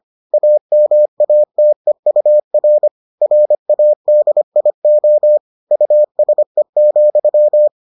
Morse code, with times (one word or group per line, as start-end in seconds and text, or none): 0.3-2.9 AMATEUR
3.2-5.4 RADIO
5.7-7.7 USE,